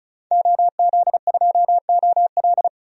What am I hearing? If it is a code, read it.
Morse code, heard as a keyed tone, about 34 wpm, 700 Hz. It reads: OZ2OL
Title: DXCC-100 Call Signs - 5 Characters - 35wpm